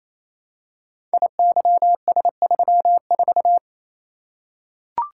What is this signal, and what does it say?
Morse code, heard as a keyed tone, about 28 wpm, 700 Hz. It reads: IYS34
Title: Code Groups - 5 Letter:Numbers - 28wpm